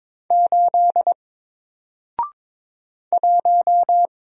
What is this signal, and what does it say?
Morse code, heard as a keyed tone, about 22 wpm, 700 Hz. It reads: 8 1
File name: Single-Letter-Number-22wpm